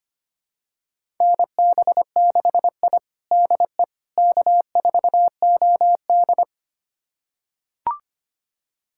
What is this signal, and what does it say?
Morse code, heard as a keyed tone, about 25 wpm, 700 Hz. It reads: NB6I DE K4OD
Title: USA Callsign DE Callsign - 4 Characters - 25wpm